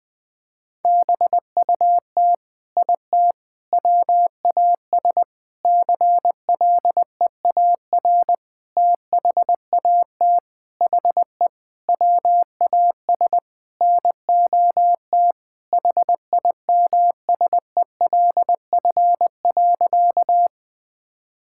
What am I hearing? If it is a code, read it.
Morse code, heard as a keyed tone, about 20 wpm, 700 Hz. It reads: BUT IT WAS CLEAR THAT HE WAS NOT HIMSELF.
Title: Sentences from Top 1600 Words 20wpm